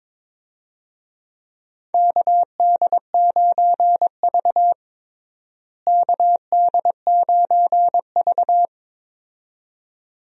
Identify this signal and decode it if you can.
Morse code, heard as a keyed tone, about 22 wpm, 700 Hz. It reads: KD9V KD9V
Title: USA Call Signs - 4 Characters - Repeated 2x - 22wpm